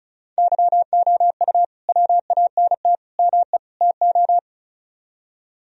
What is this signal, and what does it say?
Morse code, heard as a keyed tone, about 46 wpm, 700 Hz. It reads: YOU WANT ME TO